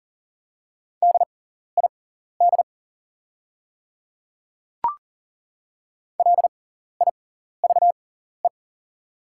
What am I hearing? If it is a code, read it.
Morse code, heard as a keyed tone, about 40 wpm, spaced as for 14 wpm, 700 Hz. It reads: DID LIVE
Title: 6x Character Spacing - Top 300 Words - 40wpm